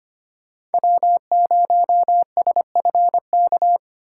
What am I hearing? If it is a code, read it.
Morse code, heard as a keyed tone, about 25 wpm, 700 Hz. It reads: W0SFK
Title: USA Call Signs - 5 Characters - 2x Character Spacing - 25wpm